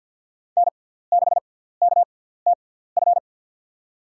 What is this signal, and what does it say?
Morse code, heard as a keyed tone, about 51 wpm, spaced as for 17 wpm, 700 Hz. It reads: N6KTF
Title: USA Call Signs - 5 Characters - 6x Character Spacing - 50wpm